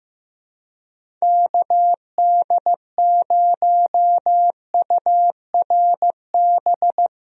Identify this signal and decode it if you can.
Morse code, heard as a keyed tone, about 15 wpm, 700 Hz. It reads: KD0URB
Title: USA Call Signs - 6 Characters - Repeated 3x - 15wpm